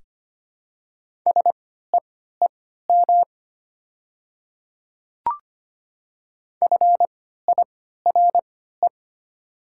Morse code, heard as a keyed tone, about 25 wpm, spaced as for 14 wpm, 700 Hz. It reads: SEEM FIRE
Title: 3x Character Spacing - Top 500 Words - 25wpm